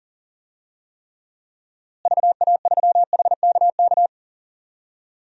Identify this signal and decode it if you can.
Morse code, heard as a keyed tone, about 40 wpm, 700 Hz. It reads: VA3HKK
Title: DXCC-100 Call Signs - 6 Characters - Repeated 2x - 40wpm